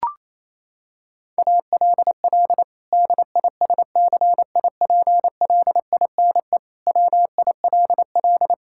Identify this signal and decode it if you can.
Morse code, heard as a keyed tone, about 27 wpm, 700 Hz. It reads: ALL DISCIPLINE WILL
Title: Sentences from Top 1000 Words - Review Part 6 - 28wpm